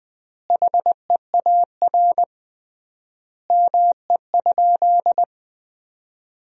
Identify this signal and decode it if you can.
Morse code, heard as a keyed tone, about 20 wpm, 700 Hz. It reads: HEAR ME?
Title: 3x Word-Spacing - Sentences from Top 300 Words - 20wpm